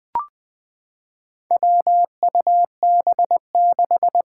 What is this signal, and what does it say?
Morse code, heard as a keyed tone, about 21 wpm, 700 Hz. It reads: WUB6
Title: Four Challenging Letter-Number Combinations 20wpm